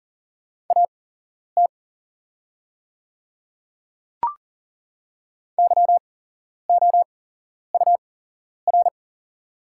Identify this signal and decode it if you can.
Morse code, heard as a keyed tone, about 40 wpm, spaced as for 11 wpm, 700 Hz. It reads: AT YOUR
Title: 8x Character Spacing - Top 100 Words - 40wpm